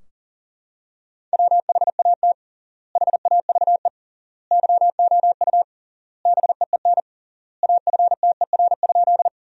Morse code, heard as a keyed tone, about 40 wpm, 700 Hz. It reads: WHAT HAVE YOU BEEN AFTER?
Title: 3x Word-Spacing - Sentences from Top 200 Words - 40wpm